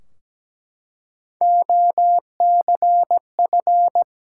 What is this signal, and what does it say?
Morse code, heard as a keyed tone, about 17 wpm, 700 Hz. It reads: OCF
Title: Top 180 QSO Elements 17wpm